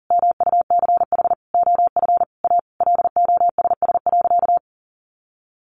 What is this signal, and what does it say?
Morse code, heard as a keyed tone, about 40 wpm, 700 Hz. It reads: MUCH OF A LOSS.
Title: Sentences from Top 1400 Words 40wpm